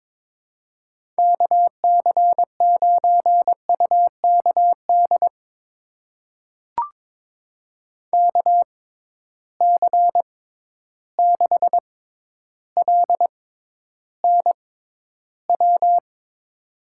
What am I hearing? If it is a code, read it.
Morse code, heard as a keyed tone, about 22 wpm, spaced as for 11 wpm, 700 Hz. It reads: KC9UKD KC6LNW
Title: USA Call Signs - 6 Characters - 6x Character Spacing - 22wpm